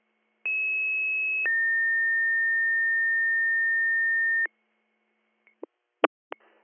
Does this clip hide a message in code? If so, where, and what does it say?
Yes: two-tone page from 0.4 to 4.5 s, heard as a one-second tone, then a three-second tone, about 2600 then 1800 Hz.